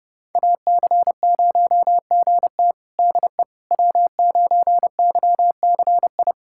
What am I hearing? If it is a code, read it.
Morse code, heard as a keyed tone, about 30 wpm, 700 Hz. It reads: AC0GT DE W9YCI
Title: USA Callsign DE Callsign - 5 Characters - 30wpm